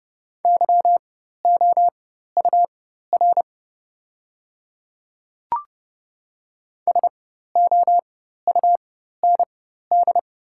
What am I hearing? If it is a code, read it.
Morse code, heard as a keyed tone, about 30 wpm, spaced as for 14 wpm, 700 Hz. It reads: YOUR SOUND
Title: 4x Character Spacing - Top 100 Words - 30wpm